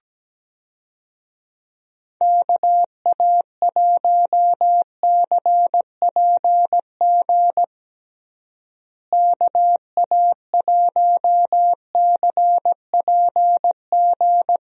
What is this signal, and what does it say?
Morse code, heard as a keyed tone, about 17 wpm, 700 Hz. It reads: KA1CPG KA1CPG
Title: USA Call Signs - 6 Characters - Repeated 2x - 17wpm